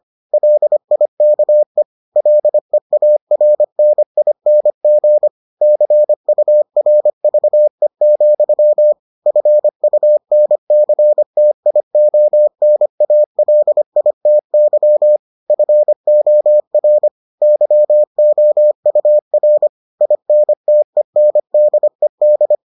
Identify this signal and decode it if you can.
Morse code, heard as a keyed tone, about 25 wpm, 600 Hz. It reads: LIKE LEARNING CURVE, FUNCTIONALITY FOR YOUR INTENDED